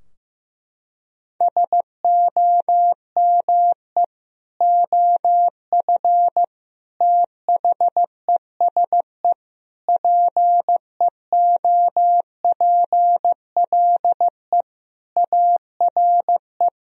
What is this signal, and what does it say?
Morse code, heard as a keyed tone, about 15 wpm, 700 Hz. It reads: SOME OF THESE PEOPLE ARE
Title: Sentences from Top 1000 Words - Review Part 3 - 15wpm